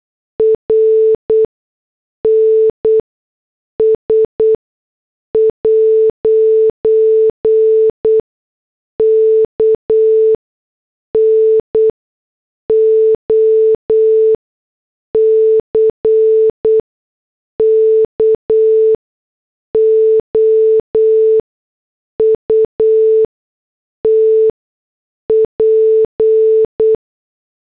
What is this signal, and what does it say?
Morse code, heard as a keyed tone, about 8 wpm, spaced as for 6 wpm, 440 Hz. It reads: RNS' KNOCKOUT P
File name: morse / headlines